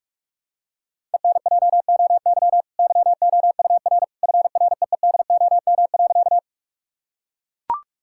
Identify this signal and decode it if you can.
Morse code, heard as a keyed tone, about 41 wpm, 700 Hz. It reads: ENJOY YOUR FREEDOM.
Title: Sentences from Top 1000 Words - Review Part 5 - 45wpm